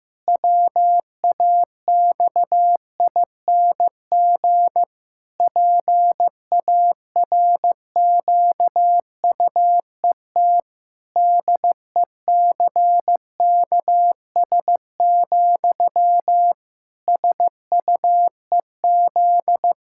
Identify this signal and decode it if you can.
Morse code, heard as a keyed tone, about 15 wpm, 700 Hz. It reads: WAXING PARQUET DECKS, SUEZ
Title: Pangram Challenge v2 - 15wpm